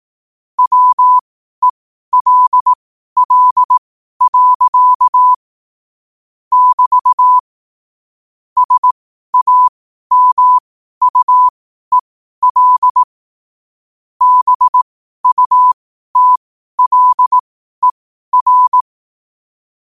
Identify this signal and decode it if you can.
Morse code, heard as a keyed tone, about 18 wpm, spaced as for 12 wpm, 1000 Hz. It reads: WELL. <BT> SAMUEL BUTLER
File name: qotdfiles / 12wpm